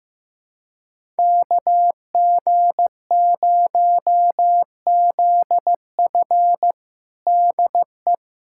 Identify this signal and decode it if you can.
Morse code, heard as a keyed tone, about 15 wpm, 700 Hz. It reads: KG0ZF DE
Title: USA Callsign DE Callsign - 5 Characters - 15wpm